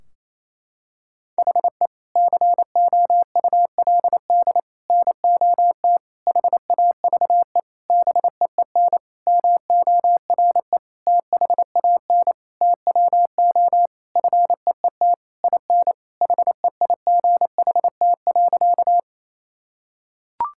Morse code, heard as a keyed tone, about 28 wpm, 700 Hz. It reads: HE COULD NOT HAVE BEEN MORE THAN TWO FEET IN HEIGHT.
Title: Sentences from Top 1600 Words 28wpm